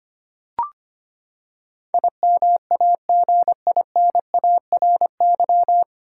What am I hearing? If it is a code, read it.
Morse code, heard as a keyed tone, about 25 wpm, 700 Hz. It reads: IMAGINARY
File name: Nine Letter Words - 25wpm